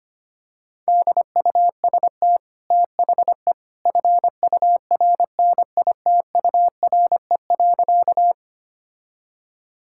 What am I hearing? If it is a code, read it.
Morse code, heard as a keyed tone, about 25 wpm, 700 Hz. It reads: DUST THE FURNITURE.
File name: Easy Sentences with Hints - 25wpm